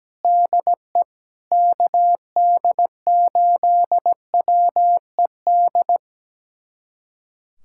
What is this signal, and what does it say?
Morse code, heard as a keyed tone, about 17 wpm, 700 Hz. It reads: DE KD8WED
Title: USA Callsign DE Callsign - 6 Characters - 17wpm